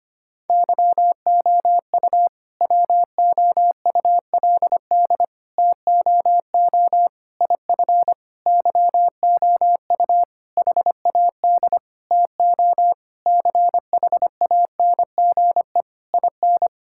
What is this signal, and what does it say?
Morse code, heard as a keyed tone, about 25 wpm, 700 Hz. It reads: YOU WOULD TOO IF YOU HAD TO CHANGE IN
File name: practice / 25wpm / All-Age Appropriate Jokes - 25wpm